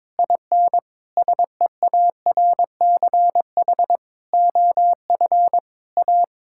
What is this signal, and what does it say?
Morse code, heard as a keyed tone, about 21 wpm, 700 Hz. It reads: IN SEARCH OF A
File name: practice / 22wpm / Sets of 7 Words - 22wpm